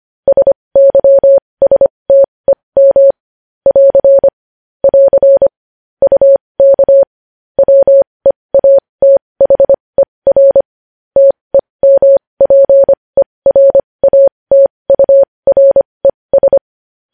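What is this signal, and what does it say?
Morse code, heard as a keyed tone, about 25 wpm, spaced as for 20 wpm, 550 Hz. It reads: SYSTEM <AR> <AR> UK WEATHER TEMPERATURES
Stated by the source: Morse code news headlines practice at 20 wpm.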